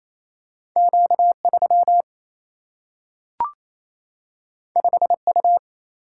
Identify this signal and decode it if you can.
Morse code, heard as a keyed tone, about 28 wpm, 700 Hz. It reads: Q3 5U
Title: Single-Letter Single-Number 28wpm